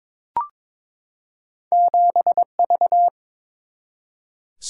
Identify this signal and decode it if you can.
Morse code, heard as a keyed tone, about 23 wpm, 700 Hz. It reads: 7V
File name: practice / 22wpm / Two Letter-Number Combinations - 22wpm